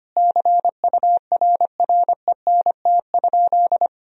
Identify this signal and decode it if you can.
Morse code, heard as a keyed tone, about 25 wpm, 700 Hz. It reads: CURRENT?
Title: Sentences from Top 1000 Words - Review Part 4 - 25wpm